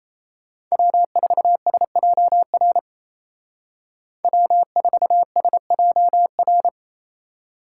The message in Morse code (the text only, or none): W4SJR W4SJR